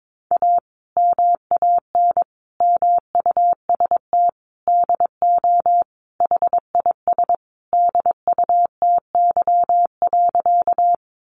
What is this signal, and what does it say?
Morse code, heard as a keyed tone, about 22 wpm, 700 Hz. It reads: A MAN MUST DO HIS DUTY.
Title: Sentences from Top 1500 Words 22wpm